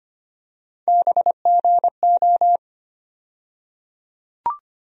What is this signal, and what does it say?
Morse code, heard as a keyed tone, about 25 wpm, 700 Hz. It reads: BGO
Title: USA Call Sign Postfixes - 3 Letters 25wpm